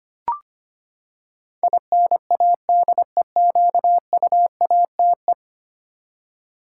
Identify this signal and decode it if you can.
Morse code, heard as a keyed tone, about 25 wpm, 700 Hz. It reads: INADEQUATE
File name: Ten Letter Words - 25wpm